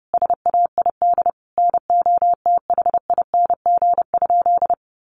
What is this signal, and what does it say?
Morse code, heard as a keyed tone, about 30 wpm, 700 Hz. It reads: SAID NOTHING?
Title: Sentences from Top 500 Words 30wpm